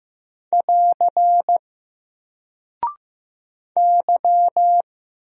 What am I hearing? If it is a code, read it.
Morse code, heard as a keyed tone, about 15 wpm, 700 Hz. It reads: <AR> Y
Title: Single Character (Letter:Number:Punctuation:Prosign) 15wpm